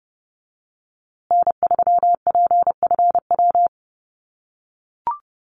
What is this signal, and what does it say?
Morse code, heard as a keyed tone, about 30 wpm, 700 Hz. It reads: N3PFW
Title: USA Call Signs Repeated 3x 30wpm